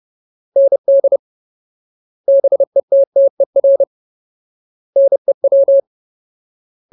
Morse code, heard as a keyed tone, about 30 wpm, 550 Hz. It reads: ND BETTER NEW
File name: Ultimate Word List / 30WPM Ultimate